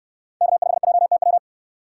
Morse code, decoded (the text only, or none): K5JTW